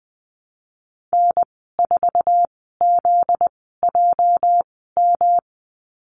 Morse code, heard as a keyed tone, about 20 wpm, spaced as for 14 wpm, 700 Hz. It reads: N4ZJM
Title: 245 Lesson 39g - Letter z - Call Signs 20wpm